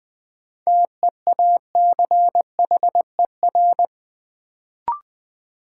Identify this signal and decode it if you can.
Morse code, heard as a keyed tone, about 21 wpm, 700 Hz. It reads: TEACHER
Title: Sentences from Top 900 Words - New Word Drill - 20wpm